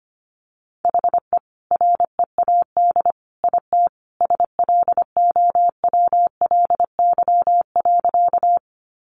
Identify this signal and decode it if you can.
Morse code, heard as a keyed tone, about 25 wpm, 700 Hz. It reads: HE READ IT SLOWLY.
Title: Sentences from Top 1100 Words 25wpm